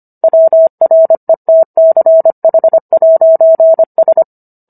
Morse code, heard as a keyed tone, about 25 wpm, 650 Hz. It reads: WRETCH'S